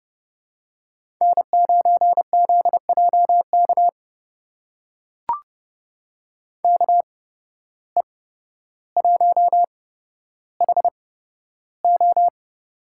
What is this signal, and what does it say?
Morse code, heard as a keyed tone, about 30 wpm, spaced as for 13 wpm, 700 Hz. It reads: N9ZJK KE1HO
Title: USA Call Signs - 5 Characters - 8x Character Spacing - 30wpm